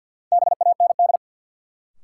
Morse code, heard as a keyed tone, about 52 wpm, 700 Hz. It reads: BAND
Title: Sentences from Top 1600 Words - New Word Drill Alt - 50wpm